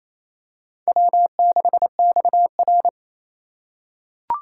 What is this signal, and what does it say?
Morse code, heard as a keyed tone, about 28 wpm, 700 Hz. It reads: W6XR
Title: K1USN SST QSO Practice - 28wpm